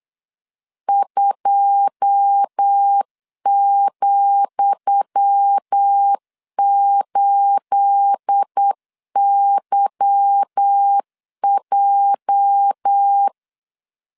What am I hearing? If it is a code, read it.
Morse code, heard as a keyed tone, about 8 wpm, 800 Hz. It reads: 2,8YJ
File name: LEZIONI / 10 SIGNS / 10D COMA 8wpm 3749